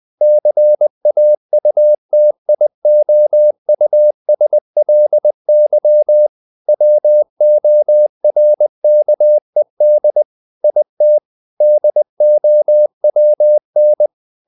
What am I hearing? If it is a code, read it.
Morse code, heard as a keyed tone, about 20 wpm, 600 Hz. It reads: CAUTIOUSLY WORKED IT DOWN
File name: the old man and the sea / the old man and the sea part 1 20wpm